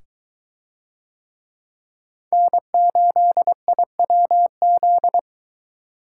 Morse code, heard as a keyed tone, about 21 wpm, 700 Hz. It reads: N8IWZ